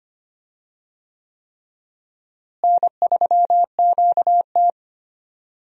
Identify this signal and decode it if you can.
Morse code, heard as a keyed tone, about 25 wpm, 700 Hz. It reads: N3QT
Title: USA Call Signs - 4 Characters - Repeated 3x - 25wpm